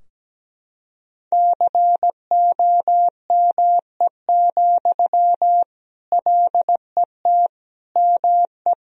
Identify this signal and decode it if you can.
Morse code, heard as a keyed tone, about 17 wpm, 700 Hz. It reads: COME, LET ME
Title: Sentences from Top 1000 Words - Review Part 2 - 17wpm